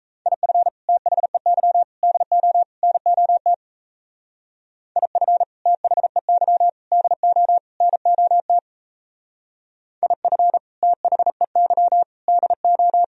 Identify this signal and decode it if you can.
Morse code, heard as a keyed tone, about 37 wpm, 700 Hz. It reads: IF THEY DO NOT IF THEY DO NOT IF THEY DO